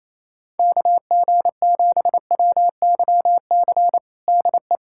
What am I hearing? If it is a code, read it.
Morse code, heard as a keyed tone, about 27 wpm, 700 Hz. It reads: KG7WYC DE
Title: USA Callsign DE Callsign 28wpm